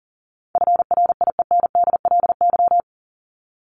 Morse code, heard as a keyed tone, about 37 wpm, 700 Hz. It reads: FRIENDLY
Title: LY Suffix - Top 100 Words with Base Words 40wpm